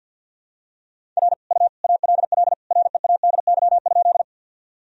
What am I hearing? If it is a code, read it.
Morse code, heard as a keyed tone, about 52 wpm, 700 Hz. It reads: R U ALL READY?
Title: Sentences from Top 400 Words and Abbreviations 50wpm